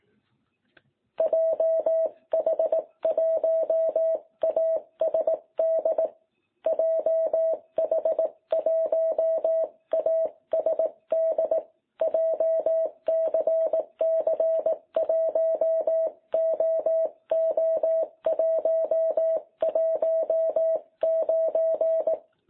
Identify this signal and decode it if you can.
Morse code, heard as a keyed tone, about 18 wpm, 660 Hz. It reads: JH1ASD JH1ASD JCC1OO119
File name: jh8cbh / 2021cw